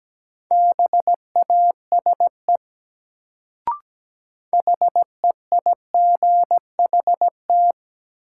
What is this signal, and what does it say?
Morse code, heard as a keyed tone, about 17 wpm, 700 Hz. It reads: BASE HEIGHT
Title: Sentences from Top 1600 Words - New Word Drill Alt - 17wpm